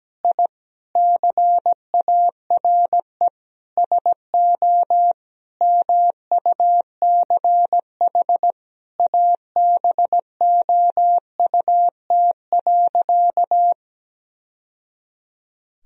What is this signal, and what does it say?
Morse code, heard as a keyed tone, about 17 wpm, 700 Hz. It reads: I CARE SO MUCH ABOUT.
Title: Sentences from Top 1000 Words - Review Part 2 - 17wpm